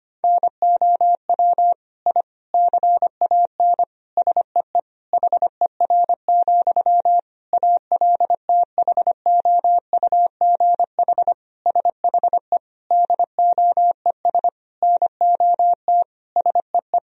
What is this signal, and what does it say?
Morse code, heard as a keyed tone, about 25 wpm, 700 Hz. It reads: NOW I CAN SEE HER, ALTHOUGH SHE DOES NOT SEE
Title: Sentences from Top 1000 Words - Review Part 3 - 25wpm